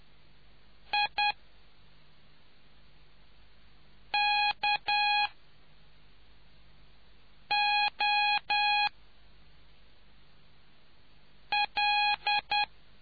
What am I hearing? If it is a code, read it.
Morse code, heard as a keyed tone, about 10 wpm, spaced as for 3 wpm, 830 Hz. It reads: IKOL